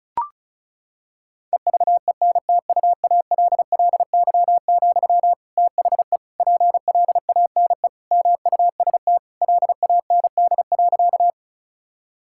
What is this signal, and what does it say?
Morse code, heard as a keyed tone, about 36 wpm, 700 Hz. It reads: EVENTUALLY, THE PLANE MUST LAND.
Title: Sentences from Top 1000 Words - Review Part 6 - 35wpm